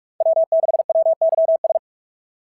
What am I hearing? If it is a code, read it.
Morse code, heard as a keyed tone, about 44 wpm, 690 Hz. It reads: W6WYS
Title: SOTA QSO Practice 45wpm